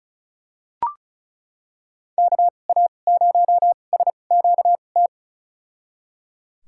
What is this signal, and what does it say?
Morse code, heard as a keyed tone, about 34 wpm, spaced as for 25 wpm, 700 Hz. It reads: KA0SQT
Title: USA Call Signs - 2x Character Spacing - 35wpm